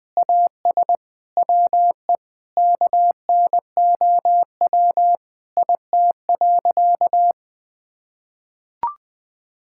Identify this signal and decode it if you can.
Morse code, heard as a keyed tone, about 20 wpm, 700 Hz. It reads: AS WE KNOW IT.
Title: Sentences with 8-Word Phrases - Spoken Beginning - 20wpm